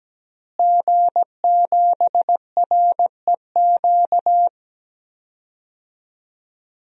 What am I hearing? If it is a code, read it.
Morse code, heard as a keyed tone, about 17 wpm, 700 Hz. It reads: G7REQ
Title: DXCC-100 Call Signs - 5 Characters - Repeated 3x - 17wpm